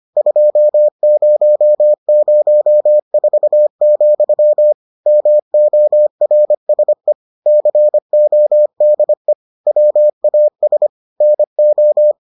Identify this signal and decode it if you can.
Morse code, heard as a keyed tone, about 25 wpm, 600 Hz. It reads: 2004, MORSE CODE WAS NO